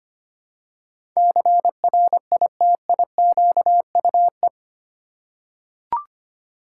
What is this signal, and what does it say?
Morse code, heard as a keyed tone, about 25 wpm, 700 Hz. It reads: CRITIQUE
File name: 8 Letter Words - First 3 Letters Given - 25wpm